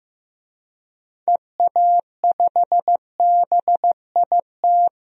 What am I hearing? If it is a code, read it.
Morse code, heard as a keyed tone, about 16 wpm, 700 Hz. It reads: EA5BIT
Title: DXCC-100 Call Signs - 6 Characters - Repeated 2x - 15wpm